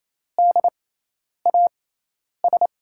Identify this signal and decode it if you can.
Morse code, heard as a keyed tone, about 28 wpm, spaced as for 10 wpm, 700 Hz. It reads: DAS
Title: USA Call Signs - 5 Characters - 6x Character Spacing - 28wpm